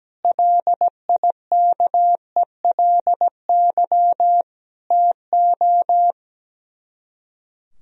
Morse code, heard as a keyed tone, about 17 wpm, 700 Hz. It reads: LIKELY TO
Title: Speed-Racing - Sets of 3 Words 17wpm